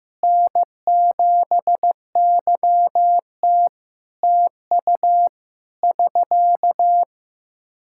Morse code, heard as a keyed tone, about 15 wpm, 700 Hz. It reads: N7YT TU <SK>
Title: Practice QSO - Exchange 4 - Ending - 15wpm